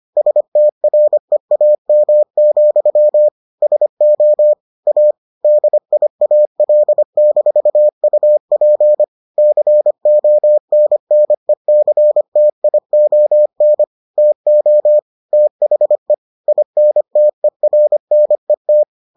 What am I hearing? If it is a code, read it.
Morse code, heard as a keyed tone, about 25 wpm, 600 Hz. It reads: STREAM, SO A DIAL-UP CONNECTION TO THE INTERNET